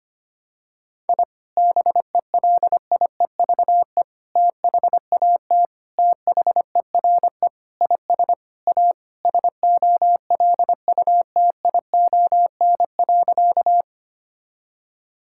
Morse code, heard as a keyed tone, about 25 wpm, 700 Hz. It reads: I BELIEVE THAT THERE IS A SOLUTION.
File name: practice / 25wpm / Sentences with 4-Word Phrases - 25wpm